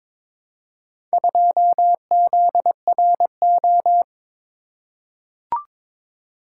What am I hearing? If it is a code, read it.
Morse code, heard as a keyed tone, about 22 wpm, 700 Hz. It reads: 2ZRO
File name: practice / 22wpm / Four Letter-Number Combinations - 22wpm